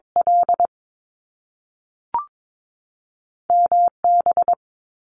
Morse code, heard as a keyed tone, about 22 wpm, spaced as for 3 wpm, 700 Hz. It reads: LM B
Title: Top Two Letter Combinations in Words 22wpm